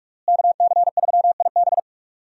Morse code, heard as a keyed tone, about 46 wpm, 700 Hz. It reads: KC3IB
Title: USA Call Signs - 5 Characters - Repeated 3x - 45wpm